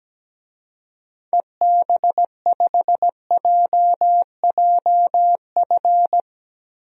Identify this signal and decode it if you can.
Morse code, heard as a keyed tone, about 17 wpm, 700 Hz. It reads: EB5JJF
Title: DXCC-100 Call Signs - 6 Characters - Repeated 2x - 17wpm